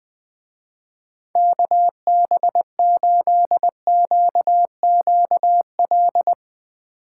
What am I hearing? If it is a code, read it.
Morse code, heard as a keyed tone, about 22 wpm, 700 Hz. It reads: KB8QQL
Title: Speed Racing - USA 2x3 Call Signs 20wpm